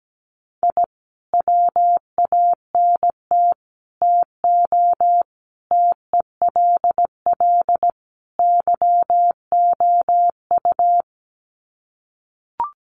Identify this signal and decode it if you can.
Morse code, heard as a keyed tone, about 17 wpm, 700 Hz. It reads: I WANT TO TELL YOU
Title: Sets of 5 Words Encore 17wpm